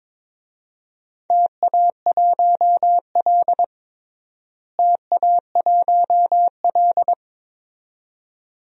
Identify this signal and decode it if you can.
Morse code, heard as a keyed tone, about 22 wpm, 700 Hz. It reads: TA1L TA1L
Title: DXCC-100 Call Signs - 4 Characters - Repeated 2x - 22wpm